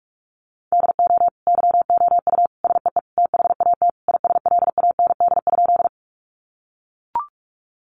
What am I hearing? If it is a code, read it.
Morse code, heard as a keyed tone, about 44 wpm, 700 Hz. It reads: DO YOU SEE THAT ISLAND?
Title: Sentences from Top 500 Words 45wpm